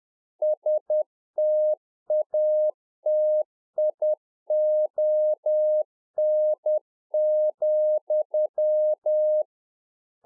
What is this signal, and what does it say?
Morse code, heard as a keyed tone, about 10 wpm, 610 Hz. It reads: STATION,
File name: UNFOGETTABLE QSO 10WPM